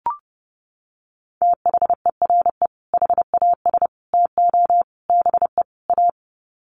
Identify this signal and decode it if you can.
Morse code, heard as a keyed tone, about 30 wpm, 700 Hz. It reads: THERE HAS TO BE A
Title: Sets of 5 Words Encore 30wpm